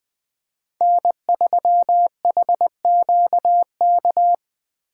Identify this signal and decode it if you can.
Morse code, heard as a keyed tone, about 20 wpm, 700 Hz. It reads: N3HQK
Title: Speed Racing - USA 1x3 Call Signs 17wpm